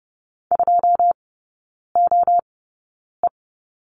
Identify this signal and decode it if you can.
Morse code, heard as a keyed tone, about 30 wpm, spaced as for 9 wpm, 700 Hz. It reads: 2OE
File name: 081 Lesson 14e - Number 2 - Three Characters 30wpm